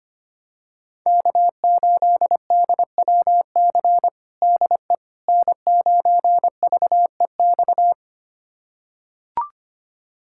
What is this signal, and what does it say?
Morse code, heard as a keyed tone, about 25 wpm, 700 Hz. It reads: K8DWC DE N9VEX
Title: USA Callsign DE Callsign - 5 Characters - 25wpm